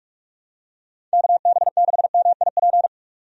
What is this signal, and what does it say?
Morse code, heard as a keyed tone, about 42 wpm, 700 Hz. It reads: KB6MIP
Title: USA Call Signs - 6 Characters - 45wpm